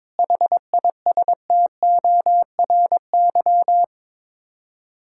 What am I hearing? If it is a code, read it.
Morse code, heard as a keyed tone, about 21 wpm, 700 Hz. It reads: HISTORY
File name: practice / 22wpm / Sets of 8 Words - 22wpm